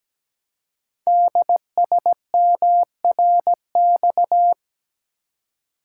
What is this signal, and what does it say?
Morse code, heard as a keyed tone, about 17 wpm, 700 Hz. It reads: DSMRX
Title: Code Groups - 5 Letters - 17wpm